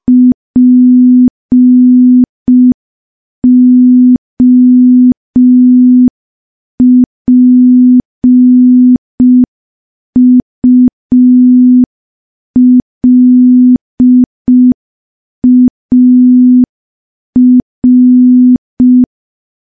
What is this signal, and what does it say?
Morse code, heard as a keyed tone, about 5 wpm, 250 Hz. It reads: POPULAR
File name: morse / morse-training-05WPM